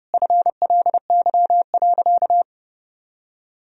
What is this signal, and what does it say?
Morse code, heard as a keyed tone, about 30 wpm, 700 Hz. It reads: FLY.
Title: Sentences from Top 1000 Words - Review Part 2 - 30wpm